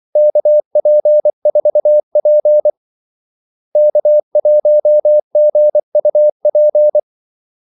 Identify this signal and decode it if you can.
Morse code, heard as a keyed tone, about 24 wpm, 600 Hz. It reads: KP4P K1GUP
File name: Callsigns-24wpm